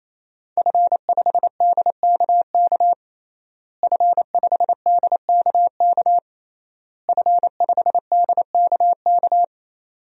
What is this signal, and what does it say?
Morse code, heard as a keyed tone, about 28 wpm, 700 Hz. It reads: F5DKK F5DKK F5DKK
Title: DXCC-100 Call Signs - 5 Characters - Repeated 3x - 28wpm